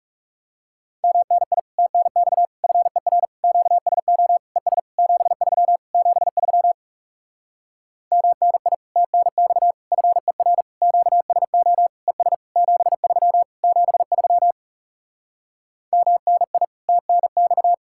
Morse code, heard as a keyed tone, about 37 wpm, 700 Hz. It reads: MNI TNX FER QSO ES 73 73 MNI TNX FER QSO ES 73 73 MNI TNX
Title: Speed-Racing Common QSO Phrases - 30wpm